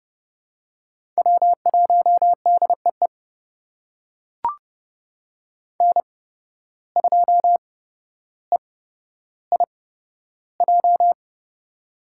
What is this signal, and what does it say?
Morse code, heard as a keyed tone, about 30 wpm, spaced as for 13 wpm, 700 Hz. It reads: W1DEE N2EIJ
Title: USA Call Signs - 5 Characters - 8x Character Spacing - 30wpm